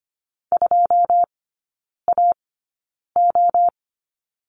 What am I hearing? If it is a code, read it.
Morse code, heard as a keyed tone, about 25 wpm, spaced as for 9 wpm, 700 Hz. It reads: 2AO
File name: Lesson 15e - Number 5 - Three Characters 25wpm